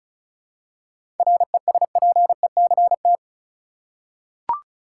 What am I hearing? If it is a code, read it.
Morse code, heard as a keyed tone, about 34 wpm, 700 Hz. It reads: RESPECT
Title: Seven Letter Words - 35wpm